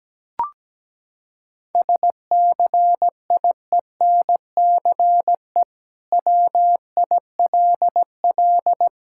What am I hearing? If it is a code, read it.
Morse code, heard as a keyed tone, about 17 wpm, 700 Hz. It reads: SCIENCE WILL
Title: Sentences from Top 1000 Words - Review Part 6 - 17wpm